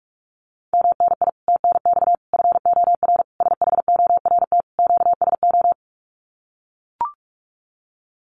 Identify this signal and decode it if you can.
Morse code, heard as a keyed tone, about 44 wpm, 700 Hz. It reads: MNI TNX FOR SHORT QSO
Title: Common QSO Phrases 45wpm